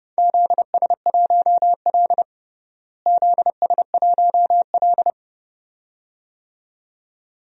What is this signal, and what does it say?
Morse code, heard as a keyed tone, about 30 wpm, 700 Hz. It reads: ZS1L ZS1L
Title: DXCC-100 Call Signs - 4 Characters - Repeated 2x - 30wpm